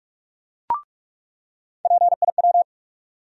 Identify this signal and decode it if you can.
Morse code, heard as a keyed tone, about 46 wpm, 700 Hz. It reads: PIW
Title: USA Call Sign Postfixes - 3 Letters 45wpm